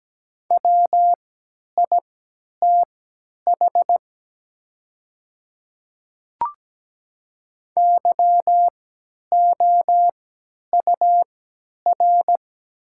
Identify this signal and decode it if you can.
Morse code, heard as a keyed tone, about 17 wpm, spaced as for 10 wpm, 690 Hz. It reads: WITH YOUR
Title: 3x Character Spacing - Top 100 Words - 17wpm